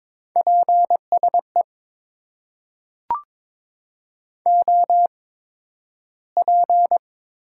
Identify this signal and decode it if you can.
Morse code, heard as a keyed tone, about 21 wpm, spaced as for 12 wpm, 700 Hz. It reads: PSE OP
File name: Top 90 QSO Elements - 8x Character Spacing - 22wpm